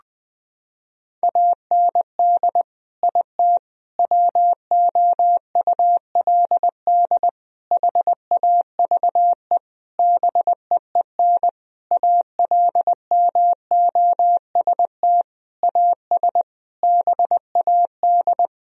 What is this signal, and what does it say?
Morse code, heard as a keyed tone, about 20 wpm, 700 Hz. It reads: AND IT WOULD HAVE BEEN ALMOST AS BAD
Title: Sentences from Top 1000 Words - Review Part 4 - 20wpm